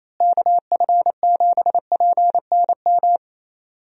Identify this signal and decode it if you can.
Morse code, heard as a keyed tone, about 28 wpm, 700 Hz. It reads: KF7PNM
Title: USA Call Signs - 6 Characters - Repeated 2x - 28wpm